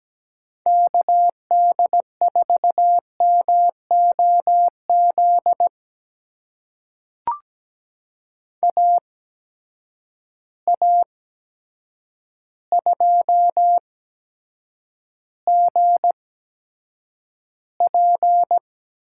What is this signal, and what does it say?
Morse code, heard as a keyed tone, about 17 wpm, spaced as for 8 wpm, 700 Hz. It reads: KD4MOZ AA2GP
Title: USA Call Signs - 8x Character Spacing - 17wpm